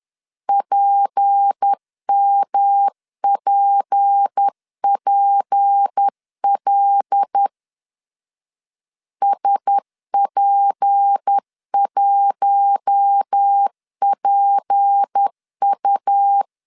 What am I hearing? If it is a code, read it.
Morse code, heard as a keyed tone, about 10 wpm, 800 Hz. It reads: PMPPL SP1PU
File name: LEZIONI / 6 PKX0 / 6 P 10wpm 2432